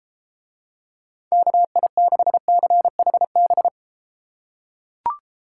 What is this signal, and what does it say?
Morse code, heard as a keyed tone, about 32 wpm, 700 Hz. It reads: KI6CHB